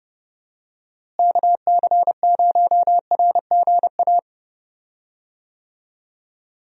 Morse code, heard as a keyed tone, about 30 wpm, 700 Hz. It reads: KC0RGA
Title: USA Call Signs - 6 Characters - Repeated 2x - 30wpm